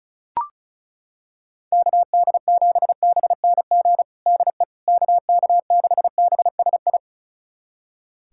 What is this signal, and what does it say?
Morse code, heard as a keyed tone, about 34 wpm, 700 Hz. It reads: KD7BNG DE KK6BSI
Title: USA Callsign DE Callsign - 6 Characters - 35wpm